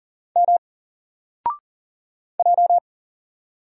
Morse code, heard as a keyed tone, about 40 wpm, 700 Hz. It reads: M J
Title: Single Letter-Number Alt 40wpm